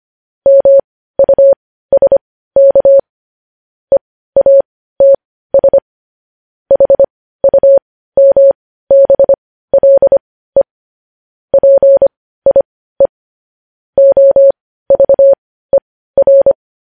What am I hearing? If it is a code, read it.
Morse code, heard as a keyed tone, about 25 wpm, spaced as for 15 wpm, 550 Hz. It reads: MUSK EATS HUMBLE PIE OVER
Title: News Headlines at 15